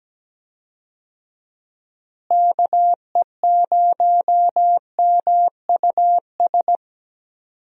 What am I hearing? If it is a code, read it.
Morse code, heard as a keyed tone, about 17 wpm, 700 Hz. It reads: KE0MUS
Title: USA Call Signs - 6 Characters - Repeated 3x - 17wpm